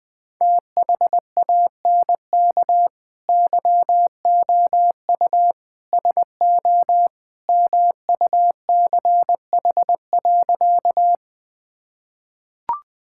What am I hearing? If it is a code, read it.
Morse code, heard as a keyed tone, about 20 wpm, 700 Hz. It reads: THANK YOU SO MUCH.
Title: Sentences with 2-Word Phrases - 20wpm